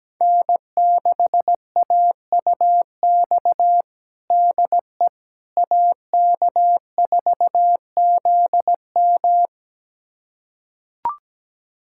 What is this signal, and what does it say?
Morse code, heard as a keyed tone, about 17 wpm, 700 Hz. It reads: N6AUX DE AK4ZM
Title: USA Callsign DE Callsign - 5 Characters - 17wpm